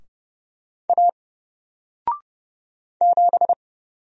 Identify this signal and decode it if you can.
Morse code, heard as a keyed tone, about 30 wpm, 700 Hz. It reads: A 7
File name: Single-Letter-Number-30wpm